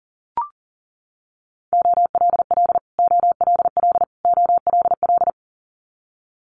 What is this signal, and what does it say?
Morse code, heard as a keyed tone, about 37 wpm, 700 Hz. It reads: OLL OLL OLL
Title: Multi-letter Phonemes Repeated 3x 40wpm